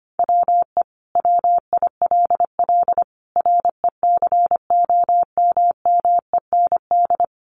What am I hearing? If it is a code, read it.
Morse code, heard as a keyed tone, about 25 wpm, 700 Hz. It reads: WE WILL RECOMMEND
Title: Sentences from Top 1200 Words 25wpm